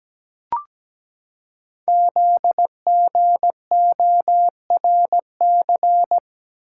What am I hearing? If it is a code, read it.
Morse code, heard as a keyed tone, about 17 wpm, 700 Hz. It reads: ZGORC
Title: Code Groups - 5 Letters - 17wpm